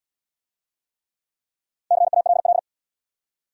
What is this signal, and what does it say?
Morse code, heard as a keyed tone, about 65 wpm, 700 Hz. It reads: CALL